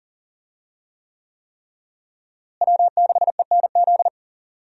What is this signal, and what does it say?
Morse code, heard as a keyed tone, about 40 wpm, 700 Hz. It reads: W6ENZ
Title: USA Call Signs - 5 Characters - Repeated 3x - 40wpm